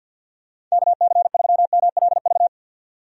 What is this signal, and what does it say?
Morse code, heard as a keyed tone, about 53 wpm, 700 Hz. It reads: KC3MLV
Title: USA Call Signs - 6 Characters - 50wpm